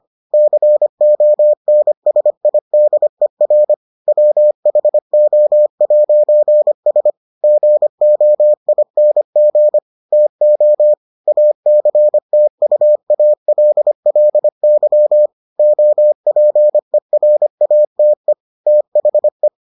Morse code, heard as a keyed tone, about 25 wpm, 600 Hz. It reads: CONSIDER WHO'S GOING TO ACTUALLY OPERATE THE